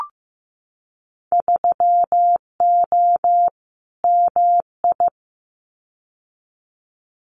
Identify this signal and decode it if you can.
Morse code, heard as a keyed tone, about 15 wpm, 700 Hz. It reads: 3O MI
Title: Winter Field Day Exchanges 15wpm